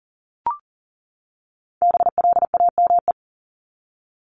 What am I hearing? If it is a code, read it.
Morse code, heard as a keyed tone, about 40 wpm, 700 Hz. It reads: BLAME
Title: Top 1751-2000 Words 40wpm